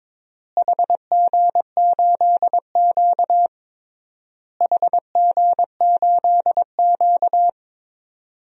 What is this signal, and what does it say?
Morse code, heard as a keyed tone, about 22 wpm, 700 Hz. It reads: HG8Q HG8Q
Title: DXCC-100 Call Signs - 4 Characters - Repeated 3x - 22wpm